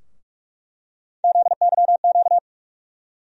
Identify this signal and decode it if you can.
Morse code, heard as a keyed tone, about 41 wpm, 700 Hz. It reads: ZYQ
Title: USA Call Sign Postfixes - 3 Letters 45wpm